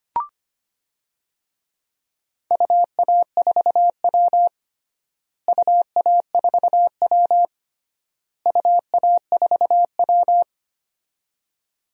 Morse code, heard as a keyed tone, about 25 wpm, 700 Hz. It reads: UA4W UA4W UA4W
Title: DXCC-100 Call Signs - 4 Characters - Repeated 3x - 25wpm